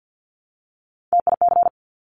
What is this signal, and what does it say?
Morse code, heard as a keyed tone, about 50 wpm, 700 Hz. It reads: TIC
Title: Top Three Letter Combinations in Words 50wpm